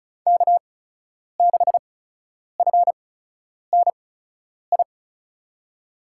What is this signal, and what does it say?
Morse code, heard as a keyed tone, about 34 wpm, spaced as for 10 wpm, 700 Hz. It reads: K6FNI
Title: USA Call Signs - 5 Characters - 8x Character Spacing - 35wpm